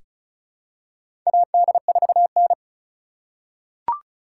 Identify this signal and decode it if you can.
Morse code, heard as a keyed tone, about 34 wpm, 700 Hz. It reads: AD4N
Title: USA Call Signs - 4 Characters - 3x Character Spacing - 35wpm